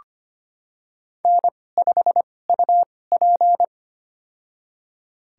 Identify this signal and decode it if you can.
Morse code, heard as a keyed tone, about 25 wpm, spaced as for 18 wpm, 700 Hz. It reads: N5UP
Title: USA Call Signs - 2x Character Spacing - 25wpm